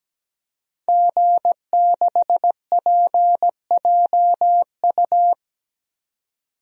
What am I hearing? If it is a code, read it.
Morse code, heard as a keyed tone, about 17 wpm, 700 Hz. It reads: G6PJU
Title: DXCC-100 Call Signs - 5 Characters - 17wpm